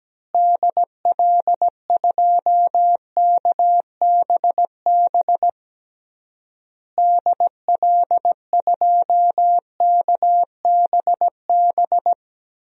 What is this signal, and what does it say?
Morse code, heard as a keyed tone, about 17 wpm, 700 Hz. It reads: DL2KBB DL2KBB
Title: DXCC-100 Call Signs - 6 Characters - Repeated 3x - 17wpm